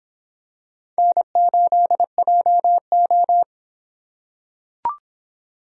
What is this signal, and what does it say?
Morse code, heard as a keyed tone, about 23 wpm, 700 Hz. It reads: N8JO